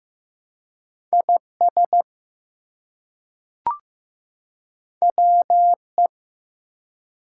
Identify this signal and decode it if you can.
Morse code, heard as a keyed tone, about 15 wpm, 700 Hz. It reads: IS WE
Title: Two Letter Words - 15wpm